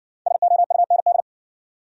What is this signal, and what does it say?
Morse code, heard as a keyed tone, about 59 wpm, 700 Hz. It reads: SOUND